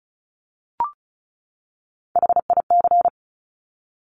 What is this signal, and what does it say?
Morse code, heard as a keyed tone, about 36 wpm, 700 Hz. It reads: HIC
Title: Top Three Letter Combinations in Words 35wpm